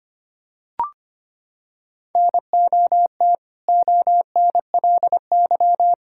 Morse code, heard as a keyed tone, about 25 wpm, 700 Hz. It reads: NOT ONLY
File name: practice / 25wpm / Sentences with 2-Word Phrases - 25wpm